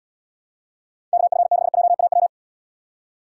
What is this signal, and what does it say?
Morse code, heard as a keyed tone, about 74 wpm, 700 Hz. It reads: KF6JAW